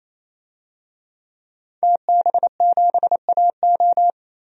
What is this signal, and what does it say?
Morse code, heard as a keyed tone, about 28 wpm, 700 Hz. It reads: TB7AO
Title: DXCC-100 Call Signs - 5 Characters - 28wpm